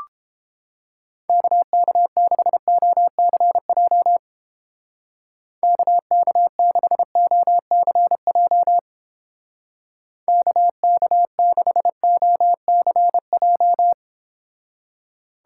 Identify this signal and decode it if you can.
Morse code, heard as a keyed tone, about 29 wpm, 700 Hz. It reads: KK6OCJ KK6OCJ KK6OCJ